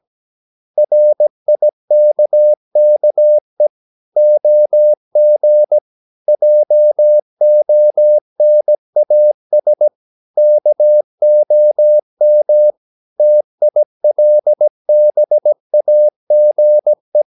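Morse code, heard as a keyed tone, about 17 wpm, 600 Hz. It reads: RIKKE OG JONAS KOM TILBAGE
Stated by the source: Slow (QRS) Morse practice at 17 wpm, from CW med Gnister.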